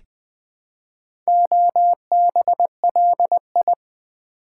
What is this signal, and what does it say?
Morse code, heard as a keyed tone, about 21 wpm, 700 Hz. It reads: OBLI
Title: Build A Word - 2 Letters At A Time - 6-Letter Words - 20wpm